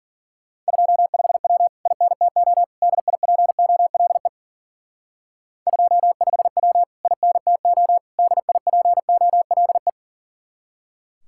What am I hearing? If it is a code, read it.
Morse code, heard as a keyed tone, about 49 wpm, 700 Hz. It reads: 25W INTO DIPOLE 25W INTO DIPOLE
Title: Speed-Racing Common QSO Phrases - 40wpm